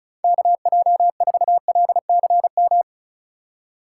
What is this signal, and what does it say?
Morse code, heard as a keyed tone, about 36 wpm, 700 Hz. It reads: KJ4LCM
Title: USA Call Signs - 6 Characters - 4x Character Spacing - 35wpm